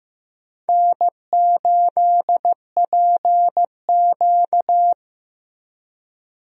Morse code, heard as a keyed tone, about 15 wpm, 700 Hz. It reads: N8PQ
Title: USA Call Signs - 4 Characters - Repeated 2x - 15wpm